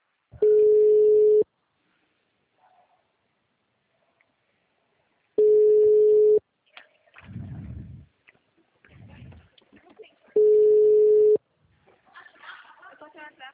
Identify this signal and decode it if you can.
Ringback tone (Europe, 425 Hz), with speech.